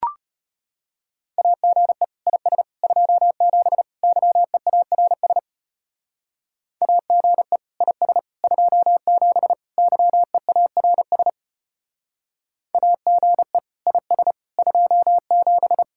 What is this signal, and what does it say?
Morse code, heard as a keyed tone, about 33 wpm, 700 Hz. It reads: AGE IS 27 YEARS AGE IS 27 YEARS AGE IS 27